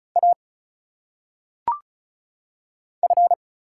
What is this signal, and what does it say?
Morse code, heard as a keyed tone, about 36 wpm, 700 Hz. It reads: A F
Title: Single Letters - Phonetic Pronunciation 35wpm